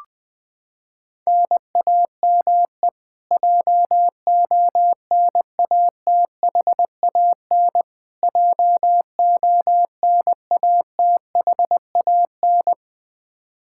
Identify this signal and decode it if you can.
Morse code, heard as a keyed tone, about 20 wpm, 700 Hz. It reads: NAME JONATHAN JONATHAN
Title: Practice QSO - Exchange 1 - Beginning - 20wpm